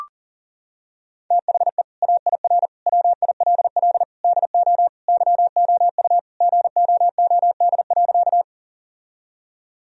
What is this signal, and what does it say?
Morse code, heard as a keyed tone, about 40 wpm, 700 Hz. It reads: THE AIR WILL DO YOU GOOD.
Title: Sentences from Top 1000 Words - Review Part 1 - 40wpm